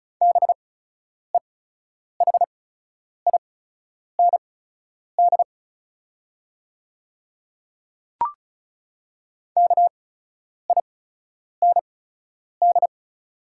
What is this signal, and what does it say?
Morse code, heard as a keyed tone, about 36 wpm, spaced as for 10 wpm, 700 Hz. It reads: BEHIND KIND
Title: 8x Character Spacing - Top 500 Words - 35wpm